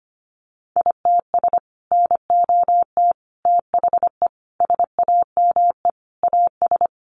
Morse code, heard as a keyed tone, about 25 wpm, 700 Hz. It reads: ITS NOT THE SAME AS
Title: Sentences from Top 600 Words 25wpm